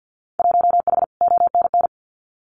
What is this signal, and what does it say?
Morse code, heard as a keyed tone, about 50 wpm, 700 Hz. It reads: 1H ONN
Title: Winter Field Day Exchanges 50wpm